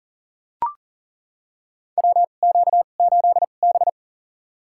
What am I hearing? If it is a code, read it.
Morse code, heard as a keyed tone, about 40 wpm, spaced as for 29 wpm, 700 Hz. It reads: WQ8B
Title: USA Call Signs - 2x Character Spacing - 40wpm